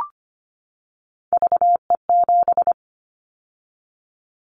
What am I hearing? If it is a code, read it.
Morse code, heard as a keyed tone, about 25 wpm, 700 Hz. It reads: VE7
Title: Callsign Prefixes for Top 100 DXCC - 25wpm